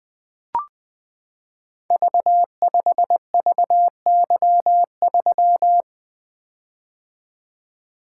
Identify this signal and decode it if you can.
Morse code, heard as a keyed tone, about 21 wpm, 700 Hz. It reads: V5VY3
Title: Code Groups - 5 Letter:Numbers - 20wpm